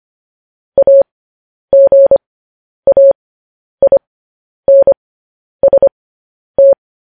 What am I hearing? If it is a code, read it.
Morse code, heard as a keyed tone, about 25 wpm, spaced as for 10 wpm, 550 Hz. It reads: AGAINST
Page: News Headlines in Morse Code at 10 WPM